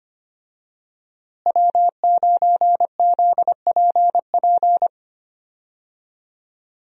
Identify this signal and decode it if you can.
Morse code, heard as a keyed tone, about 25 wpm, 700 Hz. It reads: W9ZPP
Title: USA Call Signs - 5 Characters - Repeated 3x - 25wpm